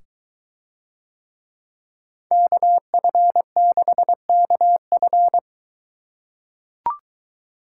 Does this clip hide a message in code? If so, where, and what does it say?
Yes: Morse code from 2.3 to 5.4 s, heard as a keyed tone, about 22 wpm, 700 Hz. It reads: KF6KF